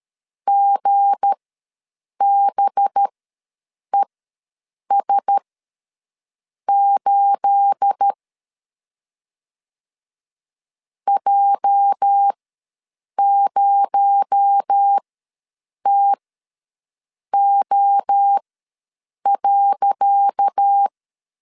Morse code, heard as a keyed tone, about 13 wpm, spaced as for 7 wpm, 800 Hz. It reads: GBES8 J0TO.
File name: LEZIONI / 10 SIGNS / 10C DOT ws 12wpm 3548